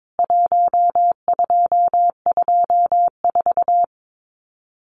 Morse code, heard as a keyed tone, about 22 wpm, 700 Hz. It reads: 1224
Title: CWT Exchange 22wpm